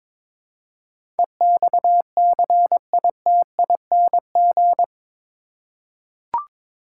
Morse code, heard as a keyed tone, about 22 wpm, 700 Hz. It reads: EXCITING
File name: practice / 22wpm / Eight Letter Words - 22wpm